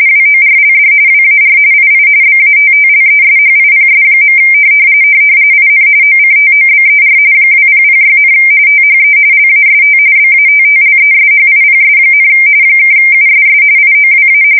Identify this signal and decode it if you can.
RTTY, about 45.45 baud. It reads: WORLD WAR II BATTLESHIPS. IN OCTOBER 2006, THE LAST BATTLESHIPS, (USS IOWA
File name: fsk170 / battleships